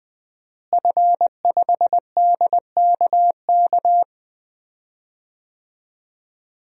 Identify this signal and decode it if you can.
Morse code, heard as a keyed tone, about 20 wpm, 700 Hz. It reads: F5DKK
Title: DXCC-100 Call Signs - 5 Characters - Repeated 3x - 20wpm